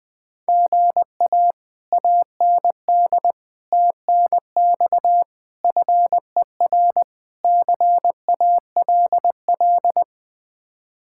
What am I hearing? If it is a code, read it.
Morse code, heard as a keyed tone, about 20 wpm, 700 Hz. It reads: GA AND TNX FER CALL
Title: Speed-Racing Common QSO Phrases Encore - 20wpm